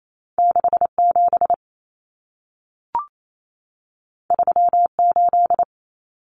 Morse code, heard as a keyed tone, about 28 wpm, 700 Hz. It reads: 67 38
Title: Common Cut Numbers - 2-Digit 28wpm